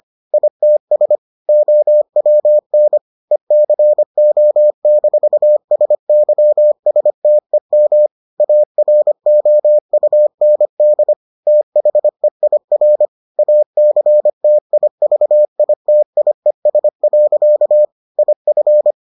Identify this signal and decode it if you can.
Morse code, heard as a keyed tone, about 25 wpm, 600 Hz. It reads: ITS OWN ECO-SYSTEM AROUND THEIR ACTIVITIES. IF